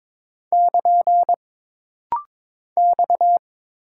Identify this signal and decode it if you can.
Morse code, heard as a keyed tone, about 22 wpm, 700 Hz. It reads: <KN> X
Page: This rapid-fire timing is geared to push your ICR (Instant Character Recognition) skills.